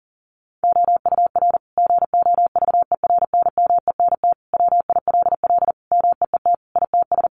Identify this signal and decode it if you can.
Morse code, heard as a keyed tone, about 40 wpm, 700 Hz. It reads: OUR GOVERNMENT WILL MEET ITS
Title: Sentences from Top 1200 Words 40wpm